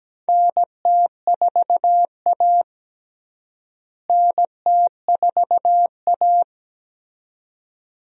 Morse code, heard as a keyed tone, about 17 wpm, 700 Hz. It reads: NT4A NT4A
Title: USA Call Signs - 4 Characters - Repeated 2x - 17wpm